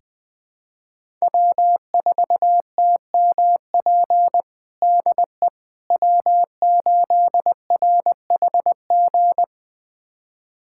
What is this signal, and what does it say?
Morse code, heard as a keyed tone, about 20 wpm, 700 Hz. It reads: W4TMP DE W8RHG
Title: USA Callsign DE Callsign - 5 Characters - 20wpm